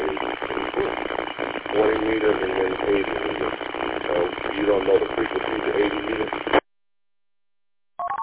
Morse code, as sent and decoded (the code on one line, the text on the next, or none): none